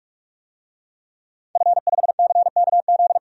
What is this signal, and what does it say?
Morse code, heard as a keyed tone, about 43 wpm, 700 Hz. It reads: F5CKZ
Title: DXCC-100 Call Signs - 5 Characters - Repeated 2x - 45wpm